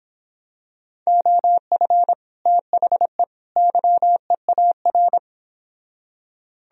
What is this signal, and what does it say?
Morse code, heard as a keyed tone, about 26 wpm, 700 Hz. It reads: OF THE YEAR